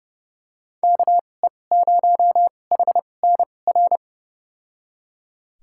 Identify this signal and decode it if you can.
Morse code, heard as a keyed tone, about 31 wpm, spaced as for 22 wpm, 700 Hz. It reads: KE0HNR
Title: USA Call Signs - 6 Characters - 2x Character Spacing - 30wpm